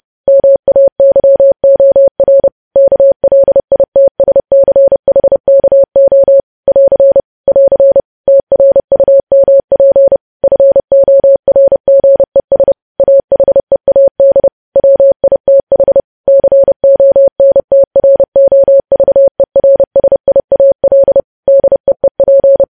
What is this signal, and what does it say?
Morse code, heard as a keyed tone, about 30 wpm, 550 Hz. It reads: MAYOR KLITSCHKO <AR> <AR> TRUMP FORGES AHEAD WITH CONTROVERSIAL DEEP
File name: podcast / News headlines-1x-30x30